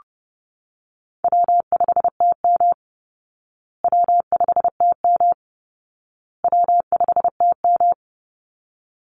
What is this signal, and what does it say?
Morse code, heard as a keyed tone, about 28 wpm, 700 Hz. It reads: W5TM W5TM W5TM
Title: Call Signs Repeated 2x - 30wpm